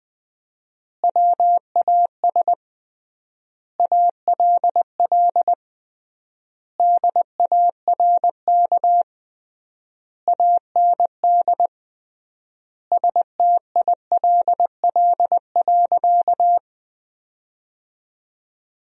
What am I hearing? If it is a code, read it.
Morse code, heard as a keyed tone, about 21 wpm, 700 Hz. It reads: WAS ALL DARK AND STILL.
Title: 3x Word-Spacing - Sentences from Top 500 Words - 20wpm